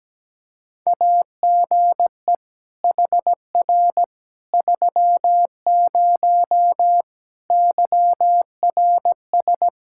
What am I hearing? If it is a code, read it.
Morse code, heard as a keyed tone, about 17 wpm, 700 Hz. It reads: AGE HR 30 YRS
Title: Practice QSO - Exchange 3 - More Info - 17wpm